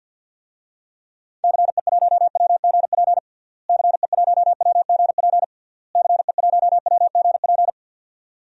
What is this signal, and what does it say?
Morse code, heard as a keyed tone, about 51 wpm, 700 Hz. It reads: CE1WGP CE1WGP CE1WGP
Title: DXCC-100 Call Signs - 6 Characters - Repeated 3x - 50wpm